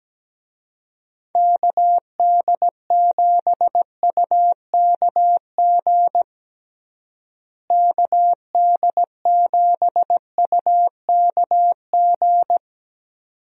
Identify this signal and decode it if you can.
Morse code, heard as a keyed tone, about 17 wpm, 700 Hz. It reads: KD7UKG KD7UKG
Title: USA Call Signs - 6 Characters - Repeated 2x - 17wpm